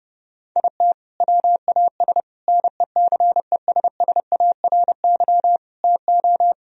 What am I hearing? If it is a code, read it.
Morse code, heard as a keyed tone, about 30 wpm, 700 Hz. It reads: IT WAS NECESSARY TO
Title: Sentences from Top 1000 Words - Review Part 1 - 30wpm